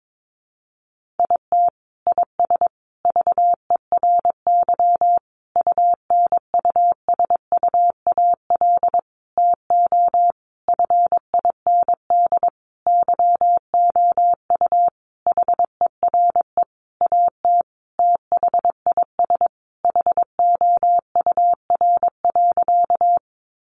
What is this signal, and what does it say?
Morse code, heard as a keyed tone, about 22 wpm, 700 Hz. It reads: IT IS VERY UNUSUAL TO FIND YOU HERE AT THIS HOUR.
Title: Sentences from Top 1400 Words 22wpm